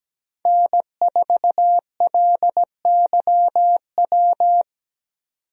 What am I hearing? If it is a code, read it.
Morse code, heard as a keyed tone, about 17 wpm, 700 Hz. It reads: N4LYW
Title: USA Call Signs - 5 Characters - 17wpm